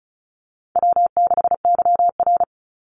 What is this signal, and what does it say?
Morse code, heard as a keyed tone, about 34 wpm, 700 Hz. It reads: W6YR
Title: Call Signs Repeated 3x - 35wpm